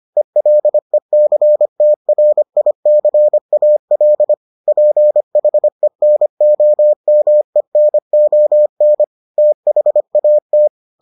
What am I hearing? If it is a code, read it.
Morse code, heard as a keyed tone, about 25 wpm, 600 Hz. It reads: ELECTRICAL PHENOMENON THAT